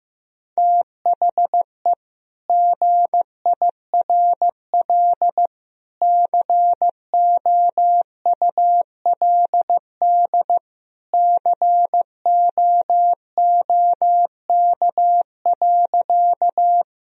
Morse code, heard as a keyed tone, about 15 wpm, 700 Hz. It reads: THE GIRL COULD COOK.
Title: Sentences from Top 1000 Words - Review Part 4 - 15wpm